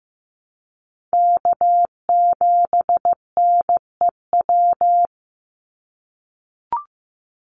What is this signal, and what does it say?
Morse code, heard as a keyed tone, about 15 wpm, 700 Hz. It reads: K7NEW
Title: NA SOTA Chaser Callsigns 15wpm